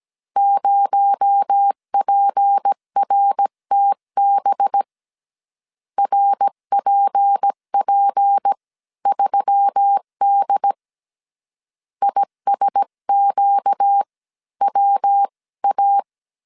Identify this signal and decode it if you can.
Morse code, heard as a keyed tone, about 17 wpm, spaced as for 14 wpm, 800 Hz. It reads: 0PRTB RPP3D ISQWA
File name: CW / DET / 1 6 1500C S C 16wpm 2733